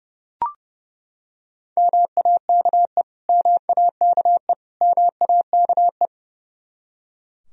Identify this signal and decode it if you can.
Morse code, heard as a keyed tone, about 30 wpm, 700 Hz. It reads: MAKE MAKE MAKE
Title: Top 100 Words Repeated 30wpm